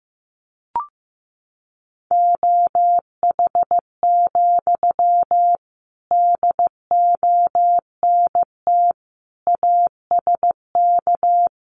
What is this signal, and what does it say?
Morse code, heard as a keyed tone, about 15 wpm, 700 Hz. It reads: OH, DONT ASK
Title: Sentences from Top 500 Words 15wpm